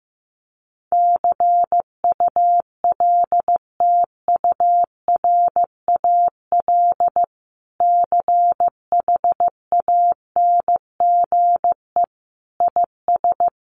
Morse code, heard as a keyed tone, about 15 wpm, 700 Hz. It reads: CULTURAL CHANGE IS
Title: Sentences from Top 1500 Words 15wpm